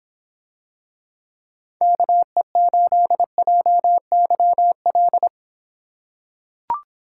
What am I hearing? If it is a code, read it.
Morse code, heard as a keyed tone, about 26 wpm, 700 Hz. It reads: KE8JYL